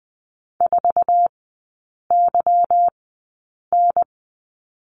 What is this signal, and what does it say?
Morse code, heard as a keyed tone, about 20 wpm, spaced as for 8 wpm, 700 Hz. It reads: 4YN
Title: 152 Lesson 25e - Letter p - Three Characters 20wpm